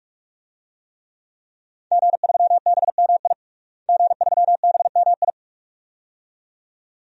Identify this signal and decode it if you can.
Morse code, heard as a keyed tone, about 46 wpm, 700 Hz. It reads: G3BMI G3BMI
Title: DXCC-100 Call Signs - 5 Characters - Repeated 2x - 45wpm